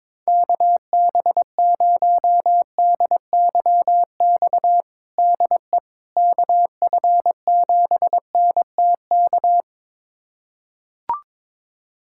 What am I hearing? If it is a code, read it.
Morse code, heard as a keyed tone, about 22 wpm, 700 Hz. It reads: KB0DYX DE KF7NTK
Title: USA Callsign DE Callsign - 6 Characters - 22wpm